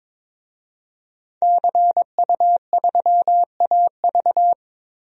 Morse code, heard as a keyed tone, about 22 wpm, 700 Hz. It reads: CU3AV
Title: DXCC-100 Call Signs - 5 Characters - Repeated 2x - 22wpm